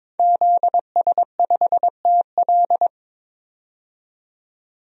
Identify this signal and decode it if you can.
Morse code, heard as a keyed tone, about 22 wpm, 700 Hz. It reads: ZS5TL
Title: DXCC-100 Call Signs - 5 Characters - 22wpm